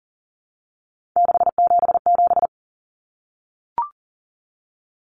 Morse code, heard as a keyed tone, about 40 wpm, 700 Hz. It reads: 677
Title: Numbers - 3-Digit 40wpm